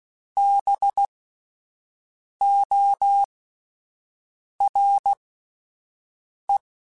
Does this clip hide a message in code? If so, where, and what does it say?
Yes: Morse code from 0.4 to 6.6 s, heard as a keyed tone, about 16 wpm, spaced as for 5 wpm, 800 Hz. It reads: BORE